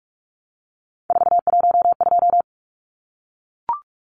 Morse code, heard as a keyed tone, about 44 wpm, 700 Hz. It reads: 412
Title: Numbers - 3-Digit 45wpm